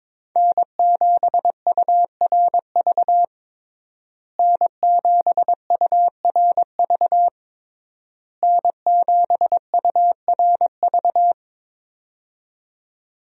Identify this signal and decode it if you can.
Morse code, heard as a keyed tone, about 22 wpm, 700 Hz. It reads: N7URV N7URV N7URV
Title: USA Call Signs - 5 Characters - Repeated 3x - 22wpm